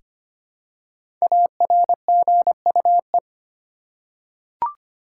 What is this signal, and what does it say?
Morse code, heard as a keyed tone, about 25 wpm, 700 Hz. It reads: ARGUE
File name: 5 Letter Words - First 3 Letters Given - 25wpm